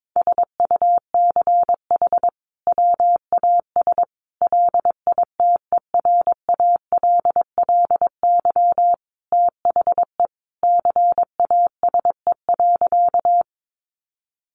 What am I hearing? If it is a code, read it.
Morse code, heard as a keyed tone, about 22 wpm, 700 Hz. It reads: SUCH WAS LITERALLY THE CASE.
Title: Sentences from Top 1400 Words 22wpm